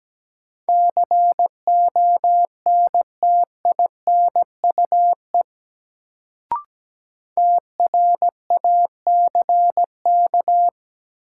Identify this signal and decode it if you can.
Morse code, heard as a keyed tone, about 17 wpm, 700 Hz. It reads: CONTINUE TRACK
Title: Sentences from Top 1100 Words - New Word Drill Alt - 17wpm